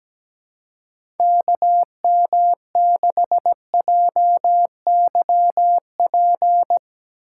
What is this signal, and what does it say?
Morse code, heard as a keyed tone, about 17 wpm, 700 Hz. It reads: KM6JYP
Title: USA Call Signs - 4x Character Spacing - 17wpm